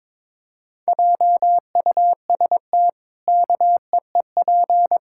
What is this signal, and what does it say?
Morse code, heard as a keyed tone, about 22 wpm, 700 Hz. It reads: JUST KEEP
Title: Pangram Challenge v2 - 22wpm